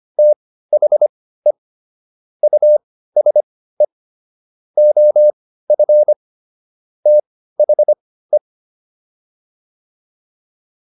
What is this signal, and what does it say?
Morse code, heard as a keyed tone, about 25 wpm, spaced as for 15 wpm, 600 Hz. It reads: THE USE OF THE